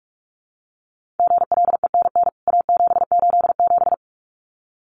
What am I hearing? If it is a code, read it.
Morse code, heard as a keyed tone, about 44 wpm, 700 Hz. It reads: GLENN A787
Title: CWT Exchange 45wpm